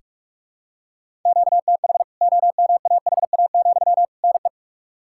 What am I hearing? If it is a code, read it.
Morse code, heard as a keyed tone, about 46 wpm, 700 Hz. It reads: QTH OMAHA, NE
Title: K1USN SST QSO Practice 45wpm